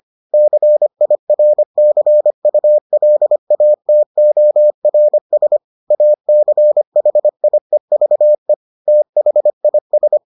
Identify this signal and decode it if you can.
Morse code, heard as a keyed tone, about 25 wpm, 600 Hz. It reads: CIRCULATORS ACHIEVE THIS